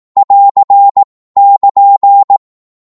Morse code, heard as a keyed tone, about 18 wpm, 800 Hz. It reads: <AR> <KN>
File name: Morse / qso-18wpm